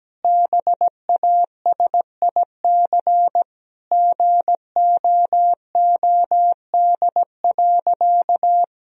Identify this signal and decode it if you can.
Morse code, heard as a keyed tone, about 17 wpm, 700 Hz. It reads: BASIC GOOD.
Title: Sentences from Top 1000 Words - Review Part 3 - 17wpm